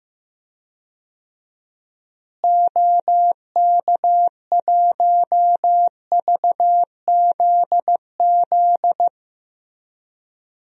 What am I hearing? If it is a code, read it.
Morse code, heard as a keyed tone, about 15 wpm, 700 Hz. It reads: OK1VZZ
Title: DXCC-100 Call Signs - 6 Characters - Repeated 2x - 15wpm